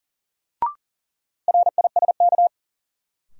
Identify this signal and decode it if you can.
Morse code, heard as a keyed tone, about 38 wpm, 700 Hz. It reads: RISK
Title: Sentences from Top 600 Words - New Word Drill - 40wpm